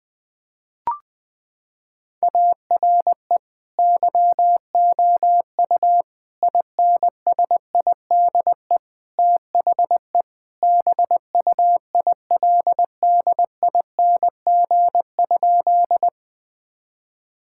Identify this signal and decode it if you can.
Morse code, heard as a keyed tone, about 20 wpm, 700 Hz. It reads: ARE YOU INSIDE THE BUILDING?
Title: Sentences from Top 1000 Words - Review Part 6 - 20wpm